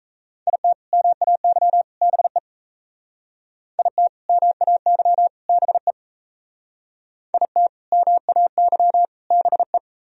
Morse code, heard as a keyed tone, about 33 wpm, 700 Hz. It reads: IT MAY BE IT MAY BE IT MAY BE
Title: Speed-Racing - Sets of 3 Words 28wpm